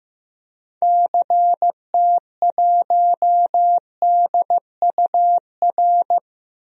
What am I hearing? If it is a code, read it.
Morse code, heard as a keyed tone, about 15 wpm, 700 Hz. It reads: CT1DUR
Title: DXCC-100 Call Signs - 6 Characters - Repeated 3x - 15wpm